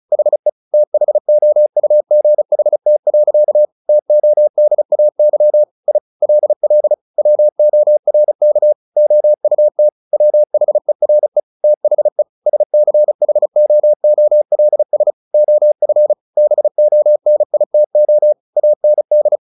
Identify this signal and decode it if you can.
Morse code, heard as a keyed tone, about 34 wpm, 600 Hz. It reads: HE THOUGHT. TODAY I LL WORK OUT WHERE THE SCHOOLS OF BONITO AND
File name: the old man and the sea / the old man and the sea 1 35wpm